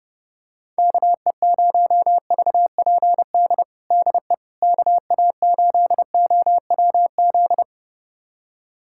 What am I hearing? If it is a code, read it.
Morse code, heard as a keyed tone, about 30 wpm, 700 Hz. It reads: KE0VPD DE KA8OWZ
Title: USA Callsign DE Callsign - 6 Characters - 30wpm